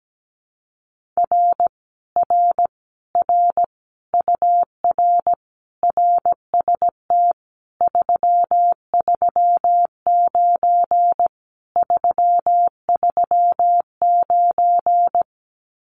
Morse code, heard as a keyed tone, about 17 wpm, 700 Hz. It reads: R R R UR RST 339 339
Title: Common QSO Phrases Encore 17wpm